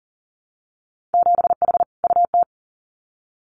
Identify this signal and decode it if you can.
Morse code, heard as a keyed tone, about 38 wpm, 700 Hz. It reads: 7H UT
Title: Winter Field Day Exchanges Encore 40wpm